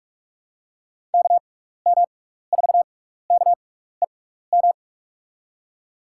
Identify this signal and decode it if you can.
Morse code, heard as a keyed tone, about 42 wpm, spaced as for 15 wpm, 700 Hz. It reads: KM4KEM
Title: USA Call Signs - 6x Character Spacing - 45wpm